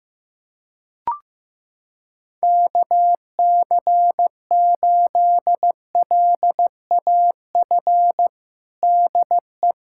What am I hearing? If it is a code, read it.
Morse code, heard as a keyed tone, about 15 wpm, 700 Hz. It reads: KC8LAF DE
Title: USA Callsign DE Callsign - 6 Characters - 15wpm